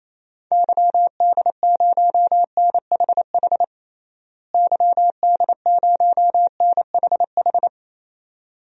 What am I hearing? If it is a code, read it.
Morse code, heard as a keyed tone, about 28 wpm, 700 Hz. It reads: YD0NHH YD0NHH
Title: DXCC-100 Call Signs - 6 Characters - Repeated 2x - 28wpm